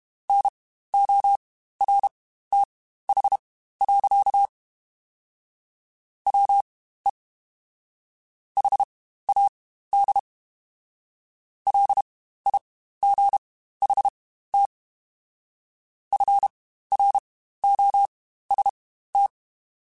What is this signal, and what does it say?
Morse code, heard as a keyed tone, about 32 wpm, spaced as for 15 wpm, 800 Hz. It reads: NORTH. WE HAD LIGHT FROST
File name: Spaced Characters / Spaced Characters-mp3s / Weather 4 24 sp @30